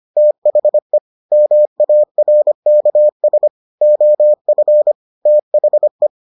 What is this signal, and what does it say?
Morse code, heard as a keyed tone, about 25 wpm, 600 Hz. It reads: THE MARKS OF THE